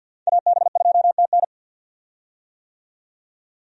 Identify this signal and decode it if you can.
Morse code, heard as a keyed tone, about 52 wpm, 700 Hz. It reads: AB2TN
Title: USA Call Signs - 5 Characters - 50wpm